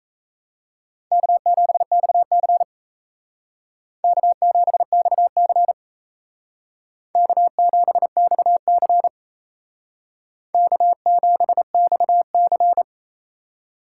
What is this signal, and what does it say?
Morse code, heard as a keyed tone, about 35 wpm, 700 Hz. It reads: K7XC K7XC K7XC K7XC